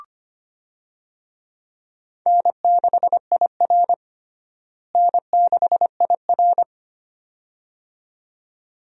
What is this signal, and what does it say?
Morse code, heard as a keyed tone, about 25 wpm, 700 Hz. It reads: N6IR N6IR
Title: USA Call Signs - 4 Characters - Repeated 2x - 25wpm